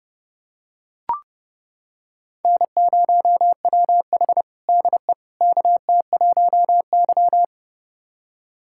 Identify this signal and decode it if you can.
Morse code, heard as a keyed tone, about 30 wpm, 700 Hz. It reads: N0WH DE KT1Y
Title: USA Callsign DE Callsign - 4 Characters - 30wpm